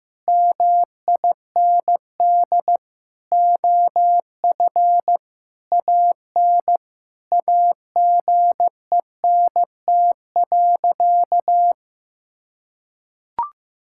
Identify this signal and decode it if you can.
Morse code, heard as a keyed tone, about 15 wpm, 700 Hz. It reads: MIND OF AN AGENT.
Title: Sentences from Top 1600 Words 15wpm